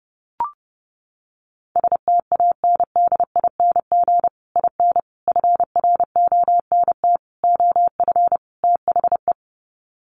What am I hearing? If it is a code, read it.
Morse code, heard as a keyed tone, about 30 wpm, 700 Hz. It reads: STANDING IN FRONT OF THE
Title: Sets of 5 Words Encore2 30wpm